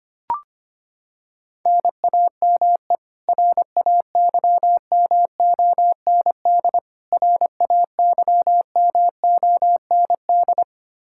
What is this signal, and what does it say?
Morse code, heard as a keyed tone, about 25 wpm, 700 Hz. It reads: NAME RAYMOND RAYMOND
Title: Practice QSO - Exchange 1 - Beginning - 25wpm